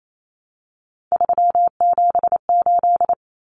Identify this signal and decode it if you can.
Morse code, heard as a keyed tone, about 28 wpm, 700 Hz. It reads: 378
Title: Common Cut Numbers - 3-Digit 28wpm